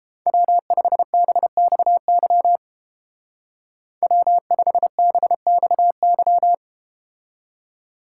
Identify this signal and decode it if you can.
Morse code, heard as a keyed tone, about 30 wpm, 700 Hz. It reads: W5BXY W5BXY